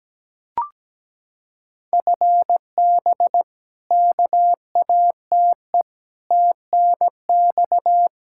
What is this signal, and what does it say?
Morse code, heard as a keyed tone, about 17 wpm, 700 Hz. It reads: FB KATE TNX
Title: Practice QSO - Exchange 3 - More Info - 17wpm